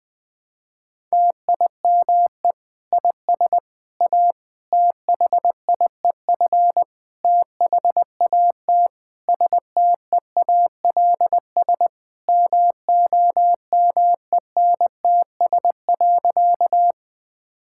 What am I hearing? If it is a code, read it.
Morse code, heard as a keyed tone, about 20 wpm, 700 Hz. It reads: TIME IS A THIEF THAT STEALS MOMENTS.
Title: Difficult Sentences with Hints - 20wpm